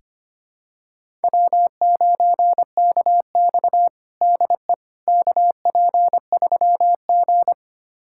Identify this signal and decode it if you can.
Morse code, heard as a keyed tone, about 25 wpm, 700 Hz. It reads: W9KX DE KP3G
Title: USA Callsign DE Callsign - 4 Characters - 25wpm